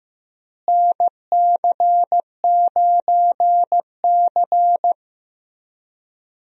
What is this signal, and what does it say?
Morse code, heard as a keyed tone, about 15 wpm, 700 Hz. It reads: NC9C
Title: USA Call Signs - 4 Characters - Repeated 3x - 15wpm